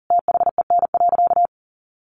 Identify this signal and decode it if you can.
Morse code, heard as a keyed tone, about 40 wpm, 700 Hz. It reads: THEN.
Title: Sentences from Top 1200 Words 40wpm